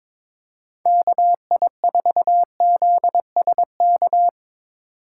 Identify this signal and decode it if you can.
Morse code, heard as a keyed tone, about 25 wpm, 700 Hz. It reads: KI4ZSK
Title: Speed Racing - USA 2x3 Call Signs 22wpm